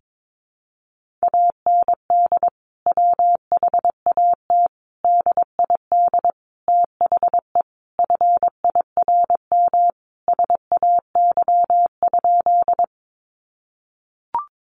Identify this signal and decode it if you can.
Morse code, heard as a keyed tone, about 22 wpm, 700 Hz. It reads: AND WHAT DID THE FIRM SAY?
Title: Sentences from Top 1400 Words 22wpm